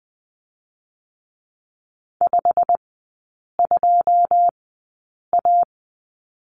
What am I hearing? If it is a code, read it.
Morse code, heard as a keyed tone, about 20 wpm, spaced as for 8 wpm, 700 Hz. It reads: 52A
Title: 172 Lesson 28e - Number 9 - Three Characters 20wpm